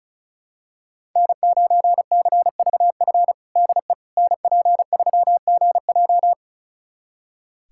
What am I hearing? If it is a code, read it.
Morse code, heard as a keyed tone, about 34 wpm, 700 Hz. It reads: N9CVF DE NP3GJ
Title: USA Callsign DE Callsign - 5 Characters - 35wpm